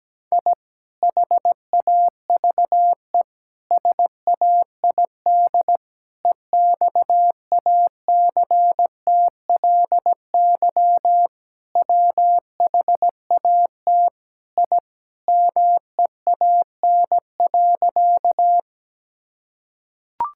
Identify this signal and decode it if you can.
Morse code, heard as a keyed tone, about 17 wpm, 700 Hz. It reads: I HAVE SAID EXACTLY WHAT I MEAN.
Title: Sentences from Top 1000 Words - Review Part 6 - 17wpm